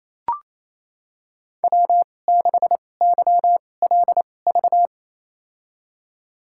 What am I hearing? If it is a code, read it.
Morse code, heard as a keyed tone, about 28 wpm, spaced as for 20 wpm, 700 Hz. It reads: W6YLV
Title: USA Call Signs - 2x Character Spacing - 28wpm